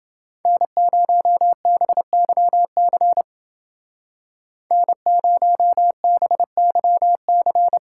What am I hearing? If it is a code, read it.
Morse code, heard as a keyed tone, about 28 wpm, 700 Hz. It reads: N0BYC N0BYC